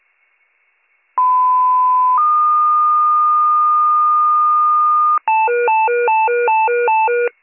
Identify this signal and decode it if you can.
Two-tone page, heard as a one-second tone, then a three-second tone, about 1010 then 1220 Hz. It is heard from 1.2 to 5.2 s.